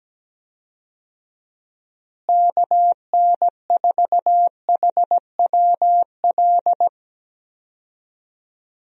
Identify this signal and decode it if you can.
Morse code, heard as a keyed tone, about 17 wpm, 700 Hz. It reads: KN4HWL
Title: USA Call Signs - 6 Characters - Repeated 3x - 17wpm